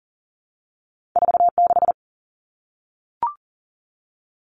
Morse code, heard as a keyed tone, about 41 wpm, 700 Hz. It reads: 4 6
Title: Two Challenging Letter-Number Combinations 40wpm